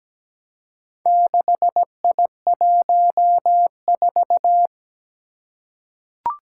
Morse code, heard as a keyed tone, about 17 wpm, 700 Hz. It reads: 6I14
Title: Four Challenging Letter-Number Combinations 17wpm